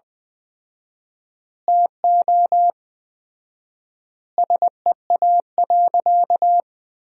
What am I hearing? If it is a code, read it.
Morse code, heard as a keyed tone, about 20 wpm, 700 Hz. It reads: TO SEA.
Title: 4x Word-Spacing - Sentences from Top 400 Words - 20wpm